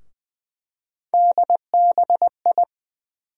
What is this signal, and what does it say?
Morse code, heard as a keyed tone, about 20 wpm, 700 Hz. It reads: DBI
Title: USA Call Sign Postfixes - 3 Letters 20wpm